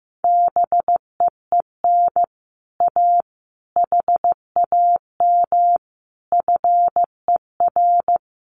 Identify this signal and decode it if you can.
Morse code, heard as a keyed tone, about 15 wpm, 700 Hz. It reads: BEEN A HAM FER
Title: Common QSO Phrases Encore 15wpm